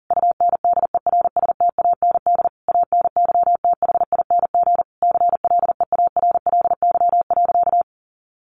Morse code, heard as a keyed tone, about 40 wpm, 700 Hz. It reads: UNDERSTAND ANYTHING CLEARLY.
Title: Sentences from Top 1300 Words 40wpm